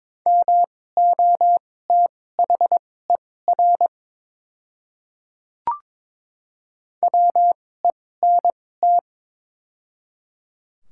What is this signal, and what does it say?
Morse code, heard as a keyed tone, about 23 wpm, spaced as for 16 wpm, 700 Hz. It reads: MOTHER WENT
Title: 2x Character Spacing - Top 200 Words - 22wpm